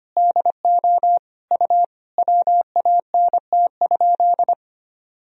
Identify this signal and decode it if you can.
Morse code, heard as a keyed tone, about 25 wpm, 700 Hz. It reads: DO U WANT?
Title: Sentences from Top 200 Words and Abbreviations 25wpm